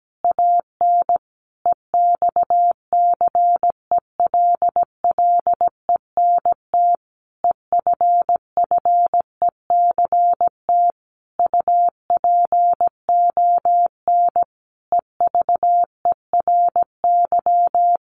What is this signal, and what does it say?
Morse code, heard as a keyed tone, about 17 wpm, 700 Hz. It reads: AN EXCELLENT EFFECT UPON EVERY
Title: Sentences from Top 1100 Words 17wpm